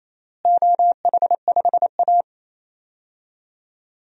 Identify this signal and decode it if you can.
Morse code, heard as a keyed tone, about 27 wpm, 700 Hz. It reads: OH5A
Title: DXCC-100 Call Signs - 4 Characters - Repeated 3x - 28wpm